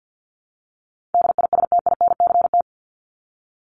Morse code, heard as a keyed tone, about 50 wpm, 700 Hz. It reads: DISTINCT
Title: Top 1501-1750 Words 50wpm